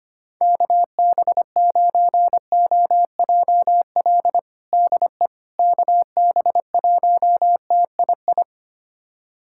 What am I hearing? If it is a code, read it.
Morse code, heard as a keyed tone, about 25 wpm, 700 Hz. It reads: KB9OJL DE KB1TII
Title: USA Callsign DE Callsign 25wpm